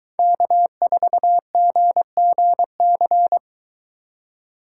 Morse code, heard as a keyed tone, about 23 wpm, 700 Hz. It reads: K4GGC